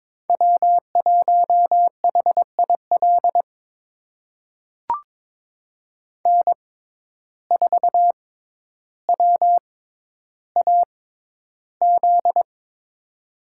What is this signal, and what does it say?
Morse code, heard as a keyed tone, about 22 wpm, spaced as for 11 wpm, 700 Hz. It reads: W1HIL N4WAZ
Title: USA Call Signs - 5 Characters - 6x Character Spacing - 22wpm